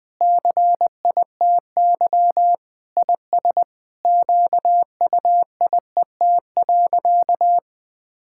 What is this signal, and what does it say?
Morse code, heard as a keyed tone, about 19 wpm, 700 Hz. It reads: CITY IS QUIET.
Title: Sentences with 7-Word Phrases - 20wpm